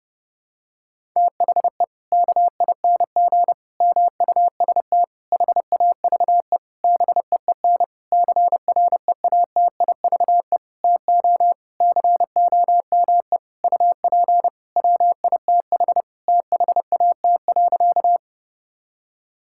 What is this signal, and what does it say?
Morse code, heard as a keyed tone, about 31 wpm, 700 Hz. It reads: THE KING MUST HAVE BEEN CREATIVE TO COME UP WITH THAT.
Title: Sentences from Top 1000 Words - Review Part 5 - 30wpm